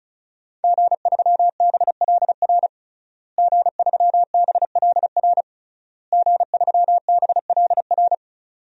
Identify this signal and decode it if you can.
Morse code, heard as a keyed tone, about 34 wpm, 700 Hz. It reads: G3BLR G3BLR G3BLR
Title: DXCC-100 Call Signs - 5 Characters - Repeated 3x - 35wpm